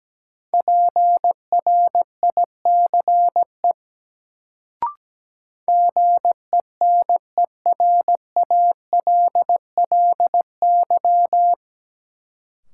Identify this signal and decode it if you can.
Morse code, heard as a keyed tone, about 17 wpm, 700 Hz. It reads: PRICE GENERALLY
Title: Sentences from Top 600 Words - New Word Drill - 17wpm